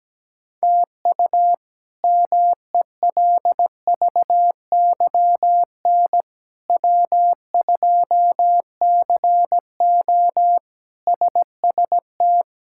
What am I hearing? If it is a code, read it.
Morse code, heard as a keyed tone, about 17 wpm, 700 Hz. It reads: TU MELVYN W2CO SST
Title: K1USN SST QSO Practice - 17wpm